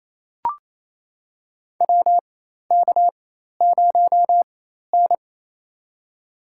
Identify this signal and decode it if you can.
Morse code, heard as a keyed tone, about 28 wpm, spaced as for 13 wpm, 700 Hz. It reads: WK0N
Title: USA Call Signs - 4 Characters - 4x Character Spacing - 28wpm